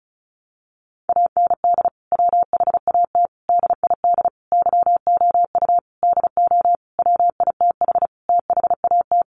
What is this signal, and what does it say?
Morse code, heard as a keyed tone, about 34 wpm, 700 Hz. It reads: AND WHAT DID YOU DO WITH THAT
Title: Sentences from Top 300 Words 35wpm